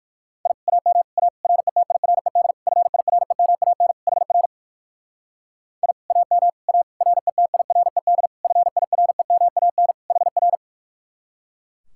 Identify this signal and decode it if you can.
Morse code, heard as a keyed tone, about 55 wpm, 700 Hz. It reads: I AM A RETIRED FIREMAN HR I AM A RETIRED FIREMAN HR
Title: Speed-Racing Common QSO Phrases Encore - 45wpm